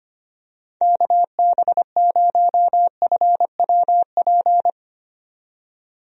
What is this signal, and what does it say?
Morse code, heard as a keyed tone, about 25 wpm, 700 Hz. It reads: KB0FWP
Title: USA Call Signs - 6 Characters - 25wpm